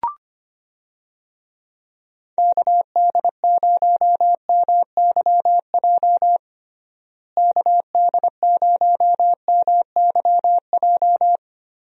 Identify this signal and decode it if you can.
Morse code, heard as a keyed tone, about 25 wpm, 700 Hz. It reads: KD0MYJ KD0MYJ
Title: USA Call Signs - 6 Characters - Repeated 2x - 25wpm